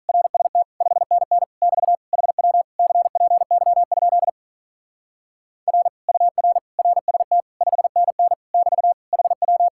Morse code, heard as a keyed tone, about 44 wpm, 700 Hz. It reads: RST 5NN <BT> HW CPY? R UR RST 5NN <BT> HW